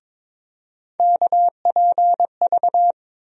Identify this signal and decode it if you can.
Morse code, heard as a keyed tone, about 22 wpm, 700 Hz. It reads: KPV
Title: USA Call Sign Postfixes - 3 Letters 22wpm